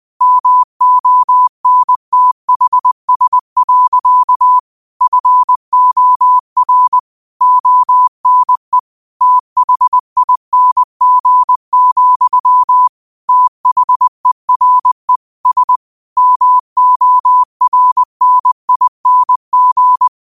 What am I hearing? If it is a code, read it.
Morse code, heard as a keyed tone, about 20 wpm, 1000 Hz. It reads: MONTHS. FOR ONE THING, THERE S MORNING
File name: qotdfiles / 20wpm